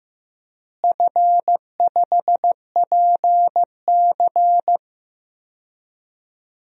Morse code, heard as a keyed tone, about 15 wpm, 700 Hz. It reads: F5PC
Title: DXCC-100 Call Signs - 4 Characters - Repeated 3x - 15wpm